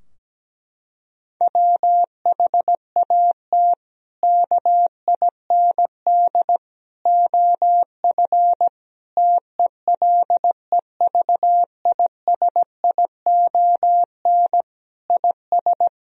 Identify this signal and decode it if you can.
Morse code, heard as a keyed tone, about 17 wpm, 700 Hz. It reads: WHAT KIND OF TELEVISION IS
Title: Sentences from Top 1000 Words - Review Part 4 - 17wpm